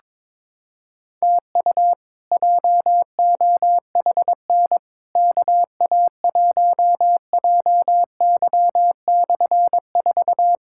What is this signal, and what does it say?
Morse code, heard as a keyed tone, about 22 wpm, 700 Hz. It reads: TU JOHN KA1JY/4
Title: K1USN SST QSO Practice - 22wpm